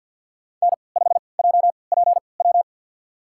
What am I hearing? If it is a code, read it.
Morse code, heard as a keyed tone, about 47 wpm, spaced as for 28 wpm, 700 Hz. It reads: N5JPW
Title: USA Call Signs - 3x Character Spacing - 50wpm